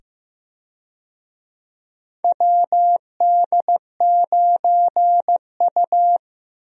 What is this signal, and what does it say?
Morse code, heard as a keyed tone, about 15 wpm, 700 Hz. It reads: WD9U
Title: USA Call Signs - 4 Characters - 15wpm